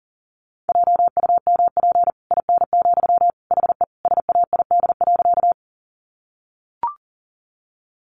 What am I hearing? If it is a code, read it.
Morse code, heard as a keyed tone, about 40 wpm, 700 Hz. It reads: JUMP IN, HE SAID.
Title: Sentences from Top 1300 Words 40wpm